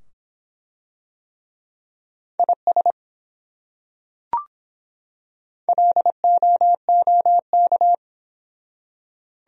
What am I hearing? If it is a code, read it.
Morse code, heard as a keyed tone, about 23 wpm, 700 Hz. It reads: IS LOOK